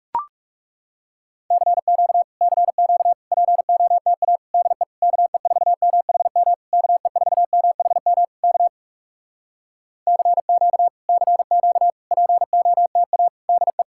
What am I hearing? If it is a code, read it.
Morse code, heard as a keyed tone, about 40 wpm, 700 Hz. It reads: CQ CQ POTA DE KE4MHM KE4MHM K CQ CQ POTA DE